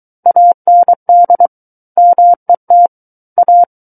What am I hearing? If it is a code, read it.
Morse code, heard as a keyed tone, about 23 wpm, 700 Hz. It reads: AND MET A
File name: Book One / 14-Chapter 14 - 23 WPM